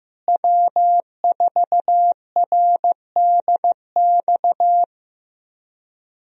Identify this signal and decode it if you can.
Morse code, heard as a keyed tone, about 15 wpm, 700 Hz. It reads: W4RDX
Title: USA Call Signs - 5 Characters - Repeated 2x - 15wpm